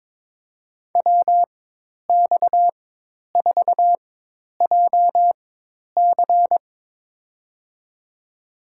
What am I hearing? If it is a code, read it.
Morse code, heard as a keyed tone, about 22 wpm, spaced as for 10 wpm, 700 Hz. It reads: WX4JC
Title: USA Call Signs - 5 Characters - 4x Character Spacing - 22wpm